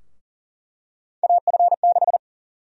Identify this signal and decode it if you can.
Morse code, heard as a keyed tone, about 37 wpm, 700 Hz. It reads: AF6
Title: USA Call Sign Prefixes - Repeated 2x - 40wpm